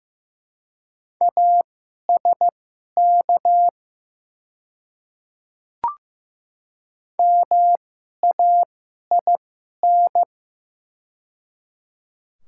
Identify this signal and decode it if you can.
Morse code, heard as a keyed tone, about 15 wpm, spaced as for 11 wpm, 700 Hz. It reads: ASK MAIN
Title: 2x Character Spacing - Top 500 Words - 15wpm